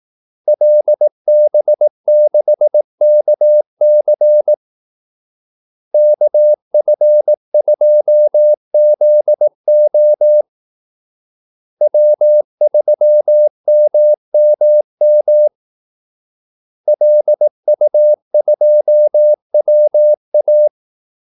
Morse code, heard as a keyed tone, about 18 wpm, 600 Hz. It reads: LB6KC KF2ZO W3MMM LU2WA
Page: Kaldesignaler 18 wpm